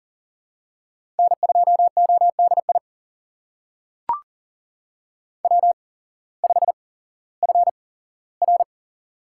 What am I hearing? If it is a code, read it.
Morse code, heard as a keyed tone, about 40 wpm, spaced as for 19 wpm, 690 Hz. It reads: N2ODI W5FR
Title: USA Call Signs - 8x Character Spacing - 40wpm